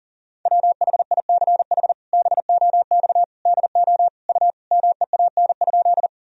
Morse code, heard as a keyed tone, about 40 wpm, 700 Hz. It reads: WHICH BOX DO U MEAN?
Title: Sentences from Top 500 Words and Abbreviations 40wpm